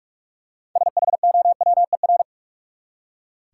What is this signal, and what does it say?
Morse code, heard as a keyed tone, about 46 wpm, 700 Hz. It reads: SHOWER
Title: Six Letter Words - 45wpm